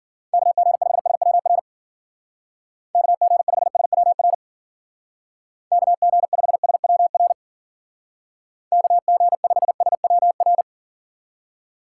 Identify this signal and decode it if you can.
Morse code, heard as a keyed tone, about 52 wpm, 700 Hz. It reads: KG5SWR KG5SWR KG5SWR KG5SWR